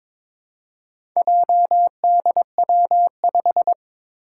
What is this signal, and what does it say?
Morse code, heard as a keyed tone, about 22 wpm, 700 Hz. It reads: JDW5
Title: Four Challenging Letter-Number Combinations 22wpm